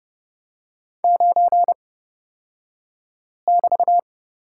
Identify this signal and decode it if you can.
Morse code, heard as a keyed tone, about 30 wpm, spaced as for 5 wpm, 700 Hz. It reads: 9<BT>
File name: Lesson 36d - BT - Two Characters 30wpm